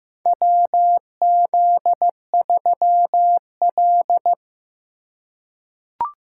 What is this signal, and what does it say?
Morse code, heard as a keyed tone, about 16 wpm, 700 Hz. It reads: WZ3L
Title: USA Call Signs - 4 Characters - 2x Character Spacing - 15wpm